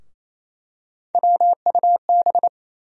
Morse code, heard as a keyed tone, about 27 wpm, 700 Hz. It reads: WUB
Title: USA Call Sign Postfixes - 3 Letters 28wpm